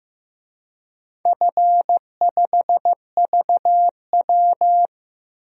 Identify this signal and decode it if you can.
Morse code, heard as a keyed tone, about 15 wpm, 700 Hz. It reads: F5VW
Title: DXCC-100 Call Signs - 4 Characters - 15wpm